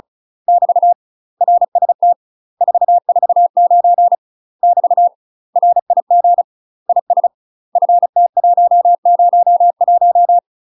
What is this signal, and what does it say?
Morse code, heard as a keyed tone, about 36 wpm, 700 Hz. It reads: <BT> RST 449 <BT> RIG IS FT101
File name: UK MASTER QSOs TRAINING / 35WPM / UK MASTER QSOs 0028